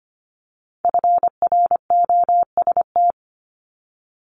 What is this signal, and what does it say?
Morse code, heard as a keyed tone, about 25 wpm, 700 Hz. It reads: FROST
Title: Multi-letter Phonemes Repeated 3x 25wpm